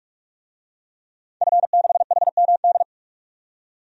Morse code, heard as a keyed tone, about 46 wpm, 700 Hz. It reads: F6HMD
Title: DXCC-100 Call Signs - 5 Characters - 45wpm